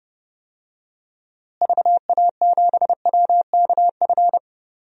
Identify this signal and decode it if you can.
Morse code, heard as a keyed tone, about 32 wpm, 700 Hz. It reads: VA7WKF
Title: DXCC-100 Call Signs - 6 Characters - Repeated 3x - 30wpm